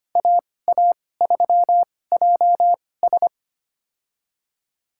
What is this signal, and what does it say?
Morse code, heard as a keyed tone, about 25 wpm, spaced as for 18 wpm, 700 Hz. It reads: AA3JS
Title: USA Call Signs - 5 Characters - 2x Character Spacing - 25wpm